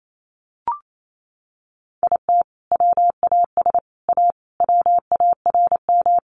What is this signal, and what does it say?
Morse code, heard as a keyed tone, about 28 wpm, 700 Hz. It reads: IT WAS A WARM
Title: Sentences from Top 500 Words 28wpm